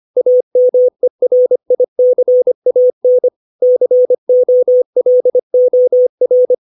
Morse code, heard as a keyed tone, about 25 wpm, 500 Hz. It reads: AMERICAN COLOR
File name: 632 words / 632 Words 25WPM